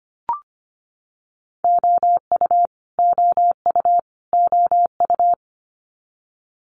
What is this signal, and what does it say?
Morse code, heard as a keyed tone, about 25 wpm, 700 Hz. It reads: OU OU OU
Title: Multi-letter Phonemes Repeated 3x 25wpm